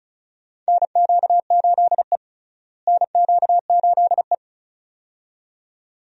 Morse code, heard as a keyed tone, about 36 wpm, 700 Hz. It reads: NQ8E NQ8E
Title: USA Call Signs - 4 Characters - Repeated 2x - 35wpm